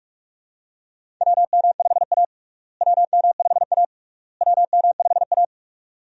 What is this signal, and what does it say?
Morse code, heard as a keyed tone, about 41 wpm, 700 Hz. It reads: WM5A WM5A WM5A
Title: USA Call Signs - 4 Characters - Repeated 3x - 45wpm